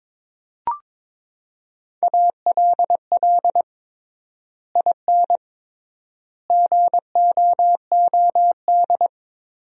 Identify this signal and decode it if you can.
Morse code, heard as a keyed tone, about 22 wpm, 700 Hz. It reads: ALL IN GOOD
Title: 3x Word-Spacing - Sentences from Top 200 Words - 22wpm